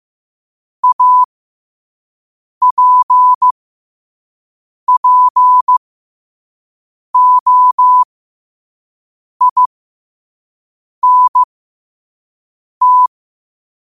Morse code, heard as a keyed tone, about 15 wpm, spaced as for 5 wpm, 1000 Hz. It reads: APPOINT
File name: qotdfiles / 5wpm